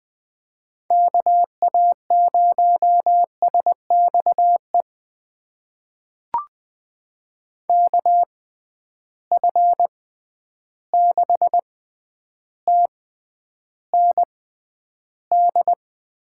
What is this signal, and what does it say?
Morse code, heard as a keyed tone, about 21 wpm, spaced as for 10 wpm, 700 Hz. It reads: KA0SXE KF6TND
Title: USA Call Signs - 6 Characters - 6x Character Spacing - 20wpm